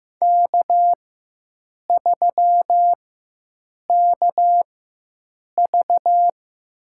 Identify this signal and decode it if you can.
Morse code, heard as a keyed tone, about 15 wpm, spaced as for 7 wpm, 700 Hz. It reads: K3KV
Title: USA Call Signs - 4 Characters - 4x Character Spacing - 15wpm